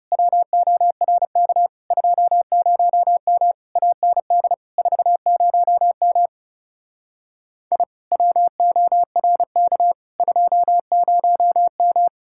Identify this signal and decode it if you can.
Morse code, heard as a keyed tone, about 39 wpm, 700 Hz. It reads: WORK 20M AND 40M I WORK 20M